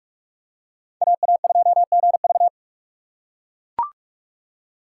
Morse code, heard as a keyed tone, about 43 wpm, 700 Hz. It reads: AA2GV
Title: USA Call Signs - 5 Characters - 3x Character Spacing - 45wpm